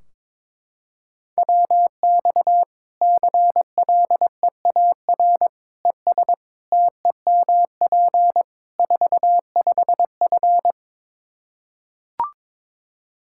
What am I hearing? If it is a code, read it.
Morse code, heard as a keyed tone, about 22 wpm, 700 Hz. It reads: WX CLEAR ES TEMP 45F
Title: Practice QSO - Exchange 2 - Info - 22wpm